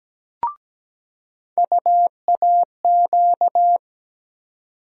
Morse code, heard as a keyed tone, about 17 wpm, 700 Hz. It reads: UAQ
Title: USA Call Sign Postfixes - 3 Letters 17wpm